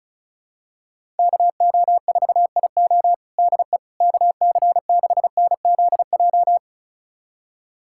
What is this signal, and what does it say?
Morse code, heard as a keyed tone, about 36 wpm, 700 Hz. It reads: KO4IO DE KC6NZJ
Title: USA Callsign DE Callsign 35wpm